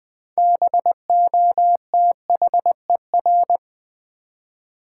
Morse code, heard as a keyed tone, about 20 wpm, 700 Hz. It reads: BOTHER
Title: 6 Letter Words - First 3 Letters Given - 20wpm